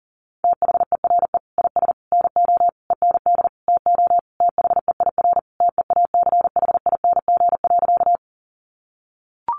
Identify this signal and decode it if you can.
Morse code, heard as a keyed tone, about 37 wpm, 700 Hz. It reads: THERE IS NO END TO THEIR TEACHING.
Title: Sentences from Top 1100 Words 40wpm